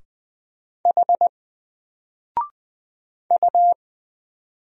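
Morse code, heard as a keyed tone, about 20 wpm, 700 Hz. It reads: H U
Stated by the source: Single letter and number Morse drill at 20 wpm, from N4LCD.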